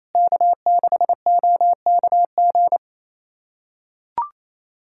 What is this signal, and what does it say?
Morse code, heard as a keyed tone, about 27 wpm, 690 Hz. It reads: K6OKG
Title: SOTA QSO Practice 28wpm